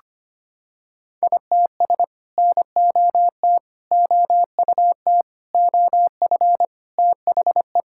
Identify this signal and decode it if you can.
Morse code, heard as a keyed tone, about 25 wpm, 700 Hz. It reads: ITS NOT OUT OF THE
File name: Sets of 9 Words - 25wpm